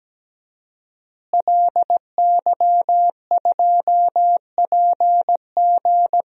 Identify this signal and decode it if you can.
Morse code, heard as a keyed tone, about 17 wpm, 700 Hz. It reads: LY2PG
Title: DXCC-100 Call Signs - 5 Characters - Repeated 2x - 17wpm